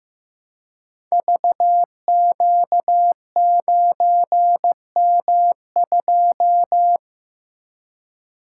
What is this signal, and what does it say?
Morse code, heard as a keyed tone, about 15 wpm, 700 Hz. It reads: VQ9M2
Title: Code Groups - 5 Letter:Numbers - 15wpm